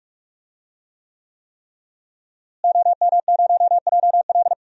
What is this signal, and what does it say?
Morse code, heard as a keyed tone, about 45 wpm, 700 Hz. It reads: OM0JL
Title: DXCC-100 Call Signs - 5 Characters - Repeated 3x - 45wpm